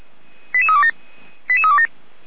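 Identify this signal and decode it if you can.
Five-tone selective call (ZVEI), heard as quick runs of five tones, tones of about 70 ms. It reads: 88327 88327
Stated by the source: ZVEI five-tone selective call, code 88241, received by SDR.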